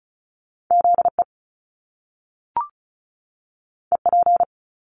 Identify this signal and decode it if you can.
Morse code, heard as a keyed tone, about 36 wpm, 700 Hz. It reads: ZE EP
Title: Top Two Letter Combinations in Words 35wpm